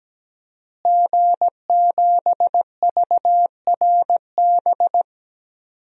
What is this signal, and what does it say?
Morse code, heard as a keyed tone, about 17 wpm, 700 Hz. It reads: G7VRB
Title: DXCC-100 Call Signs - 5 Characters - Repeated 3x - 17wpm